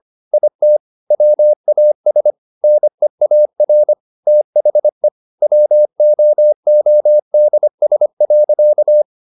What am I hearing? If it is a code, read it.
Morse code, heard as a keyed tone, about 25 wpm, 600 Hz. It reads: IT WAS NEAR THE WOODS.